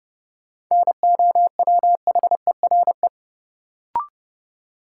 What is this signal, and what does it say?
Morse code, heard as a keyed tone, about 30 wpm, 700 Hz. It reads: NOWHERE
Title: Sets of 5 Words Encore - New Word Drill - 30wpm